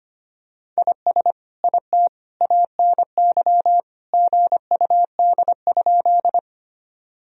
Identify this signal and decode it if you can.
Morse code, heard as a keyed tone, about 25 wpm, 700 Hz. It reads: IS IT ANY GUD?
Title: Sentences from Top 200 Words and Abbreviations 25wpm